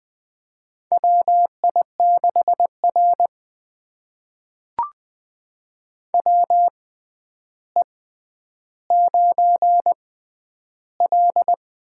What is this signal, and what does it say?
Morse code, heard as a keyed tone, about 20 wpm, spaced as for 10 wpm, 700 Hz. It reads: WI6R WE9L
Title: USA Call Signs - 4 Characters - 6x Character Spacing - 20wpm